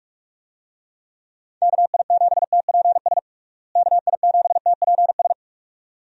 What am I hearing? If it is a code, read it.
Morse code, heard as a keyed tone, about 42 wpm, 700 Hz. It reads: KI7TPS KI7TPS
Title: USA Call Signs - 6 Characters - Repeated 2x - 45wpm